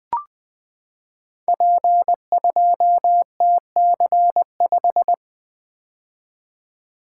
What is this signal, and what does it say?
Morse code, heard as a keyed tone, about 20 wpm, 700 Hz. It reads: P2TC5
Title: Code Groups - 5 Letter:Numbers - 20wpm